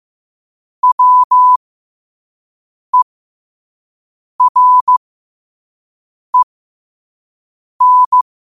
Morse code, heard as a keyed tone, about 15 wpm, spaced as for 5 wpm, 1000 Hz. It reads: WEREN
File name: qotdfiles / 5wpm